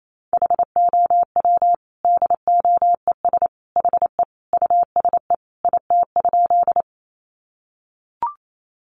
Morse code, heard as a keyed tone, about 28 wpm, 700 Hz. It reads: HOW DOES HE USE IT?
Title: Sentences from Top 200 Words 28wpm